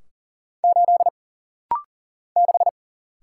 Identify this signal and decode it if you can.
Morse code, heard as a keyed tone, about 41 wpm, 700 Hz. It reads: 8 6
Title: Single Letter-Number - Rapid-Fire 40wpm